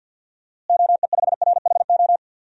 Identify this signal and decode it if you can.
Morse code, heard as a keyed tone, about 50 wpm, 700 Hz. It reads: OE5RHO
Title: DXCC-100 Call Signs - 6 Characters - Repeated 3x - 50wpm